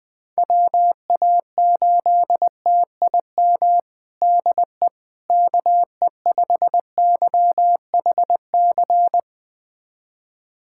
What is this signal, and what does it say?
Morse code, heard as a keyed tone, about 21 wpm, 700 Hz. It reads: WA8TIM DE KE5YHC
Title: USA Callsign DE Callsign 20wpm